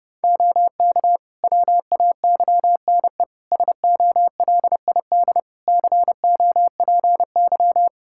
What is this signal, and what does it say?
Morse code, heard as a keyed tone, about 32 wpm, 700 Hz. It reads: OK WAYNE SOLID COPY
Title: Practice QSO - Exchange 3 - More Info - 30wpm